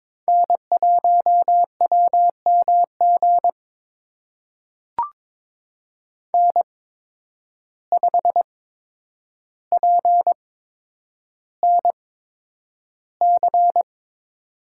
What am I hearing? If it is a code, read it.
Morse code, heard as a keyed tone, about 22 wpm, spaced as for 9 wpm, 700 Hz. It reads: N1WMG N5PNC
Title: USA Call Signs - 5 Characters - 8x Character Spacing - 22wpm